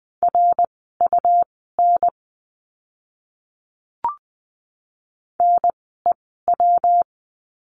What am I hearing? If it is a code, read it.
Morse code, heard as a keyed tone, about 20 wpm, spaced as for 15 wpm, 700 Hz. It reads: RUN NEW
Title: 115 Lesson 19f - Letter w - Words 20wpm